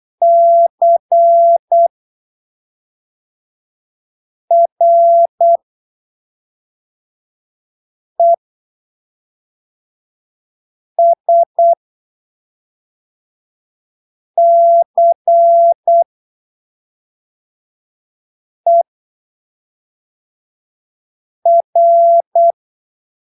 Morse code, heard as a keyed tone, about 8 wpm, spaced as for 3 wpm, 680 Hz. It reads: CRESCER
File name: arquivos / texto / texto 29 08ppm